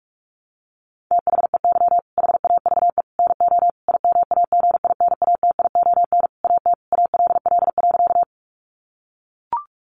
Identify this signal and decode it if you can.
Morse code, heard as a keyed tone, about 44 wpm, 700 Hz. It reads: THEY HAVE NO IMAGINATION AT ALL.
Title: Sentences from Top 1300 Words 45wpm